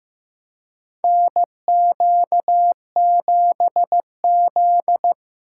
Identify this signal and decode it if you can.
Morse code, heard as a keyed tone, about 15 wpm, 700 Hz. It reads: NQ7Z
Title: USA Call Signs - 4x Character Spacing - 15wpm